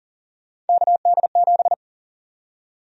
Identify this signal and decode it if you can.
Morse code, heard as a keyed tone, about 40 wpm, 700 Hz. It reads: KD7
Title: USA Call Sign Prefixes - Repeated 2x - 40wpm